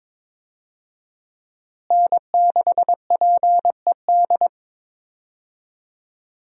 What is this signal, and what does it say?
Morse code, heard as a keyed tone, about 22 wpm, 700 Hz. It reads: N6PED
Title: USA Call Signs - 5 Characters - Repeated 2x - 22wpm